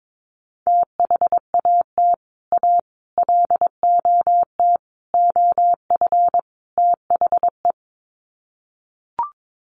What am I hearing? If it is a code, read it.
Morse code, heard as a keyed tone, about 22 wpm, 700 Hz. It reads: THAT A LOT OF THE
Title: Sets of 5 Words Encore2 22wpm